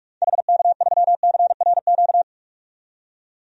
Morse code, heard as a keyed tone, about 46 wpm, 700 Hz. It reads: HK3CRQ
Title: DXCC-100 Call Signs - 6 Characters - 45wpm